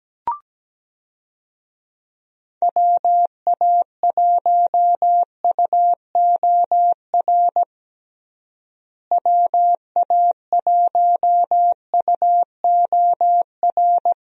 Morse code, heard as a keyed tone, about 17 wpm, 700 Hz. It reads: WA1UOR WA1UOR
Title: USA Call Signs - 6 Characters - Repeated 3x - 17wpm